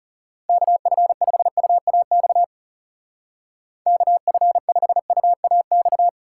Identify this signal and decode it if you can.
Morse code, heard as a keyed tone, about 36 wpm, 700 Hz. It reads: KF5UAX KF5UAX
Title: Speed Racing - USA 2x3 Call Signs 30wpm